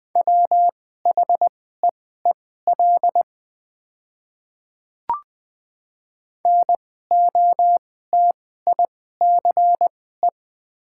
Morse code, heard as a keyed tone, about 20 wpm, spaced as for 14 wpm, 690 Hz. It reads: WHEEL NOTICE
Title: 2x Character Spacing - Top 500 Words - 20wpm